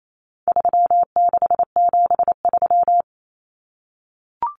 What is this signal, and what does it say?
Morse code, heard as a keyed tone, about 28 wpm, 700 Hz. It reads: 3673
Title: Common Cut Numbers - 4-Digit 28wpm